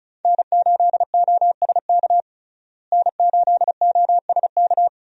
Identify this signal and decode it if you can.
Morse code, heard as a keyed tone, about 34 wpm, 700 Hz. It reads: N8OSK N8OSK
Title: USA Call Signs - 5 Characters - Repeated 2x - 35wpm